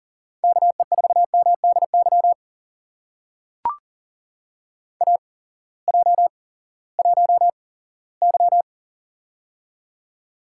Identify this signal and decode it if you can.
Morse code, heard as a keyed tone, about 40 wpm, spaced as for 20 wpm, 700 Hz. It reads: KE4MDY AJ1Y
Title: USA Call Signs - 8x Character Spacing - 40wpm